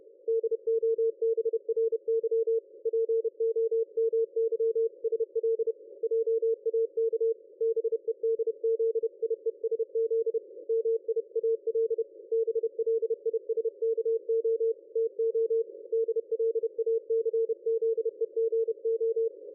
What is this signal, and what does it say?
Morse code, heard as a keyed tone, about 30 wpm, 450 Hz. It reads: DOBRY POMYSL JAK BEDZIESZ MIAL BLISKO TO DLACZEGO
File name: dlazaawansowanych / nagraniazpasma / ExpertSDR2 line out 23.01.2020 14.22.53